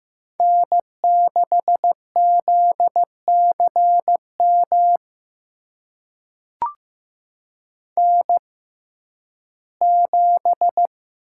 Morse code, heard as a keyed tone, about 15 wpm, spaced as for 11 wpm, 700 Hz. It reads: N6ZCM N7
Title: USA Call Signs - 6x Character Spacing - 15wpm